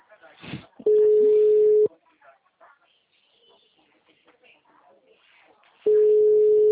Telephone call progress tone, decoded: ringback tone

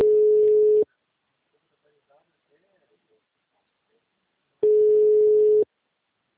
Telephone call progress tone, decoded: ringback tone

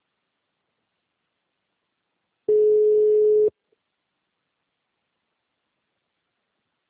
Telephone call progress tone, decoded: ringback tone

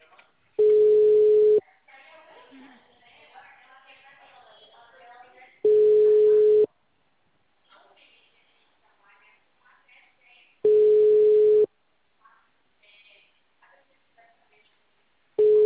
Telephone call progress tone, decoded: ringback tone

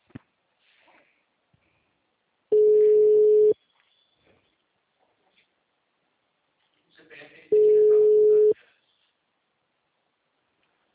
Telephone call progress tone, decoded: ringback tone

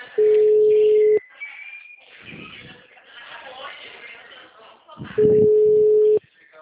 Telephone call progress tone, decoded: ringback tone

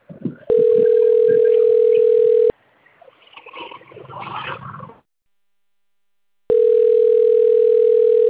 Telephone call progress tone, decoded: ringback tone

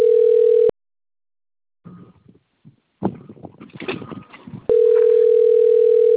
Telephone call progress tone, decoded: ringback tone

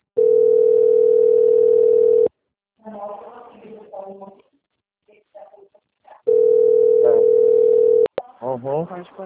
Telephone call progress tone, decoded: ringback tone

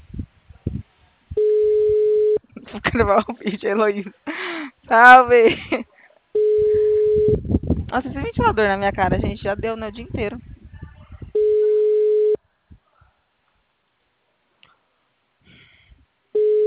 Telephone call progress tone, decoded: ringback tone

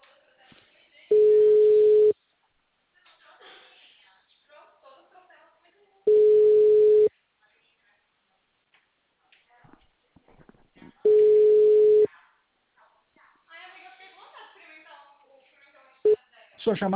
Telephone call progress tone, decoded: ringback tone

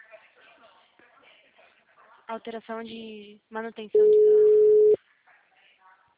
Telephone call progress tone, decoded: ringback tone

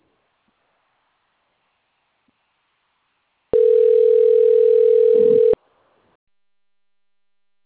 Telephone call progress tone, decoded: ringback tone

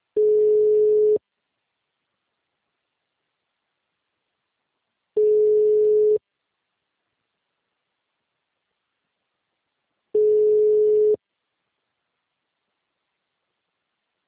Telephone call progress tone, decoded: ringback tone

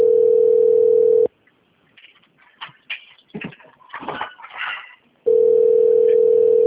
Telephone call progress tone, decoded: ringback tone

